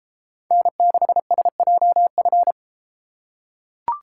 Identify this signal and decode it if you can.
Morse code, heard as a keyed tone, about 32 wpm, 700 Hz. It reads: N6SJF